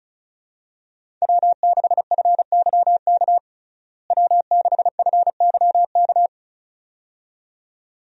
Morse code, heard as a keyed tone, about 34 wpm, 700 Hz. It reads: W6FYK W6FYK
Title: USA Call Signs - 5 Characters - Repeated 2x - 35wpm